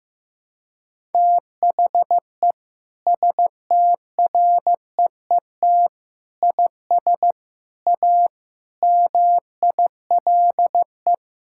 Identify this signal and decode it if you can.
Morse code, heard as a keyed tone, about 15 wpm, 700 Hz. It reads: THE STREET IS A MILE
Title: Sentences from Top 1600 Words 15wpm